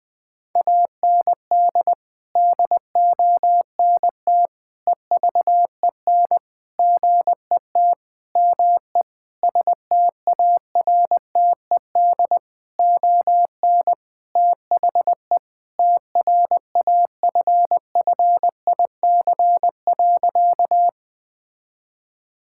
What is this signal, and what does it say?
Morse code, heard as a keyed tone, about 20 wpm, 700 Hz. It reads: AND DONT EVEN GET ME STARTED ON THE TRAFFIC.
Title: Sentences with 8-Word Phrases - Spoken Beginning - 20wpm